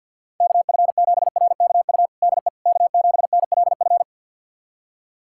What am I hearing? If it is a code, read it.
Morse code, heard as a keyed tone, about 51 wpm, 700 Hz. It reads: KF7RKU DE K7NLF
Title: USA Callsign DE Callsign 50wpm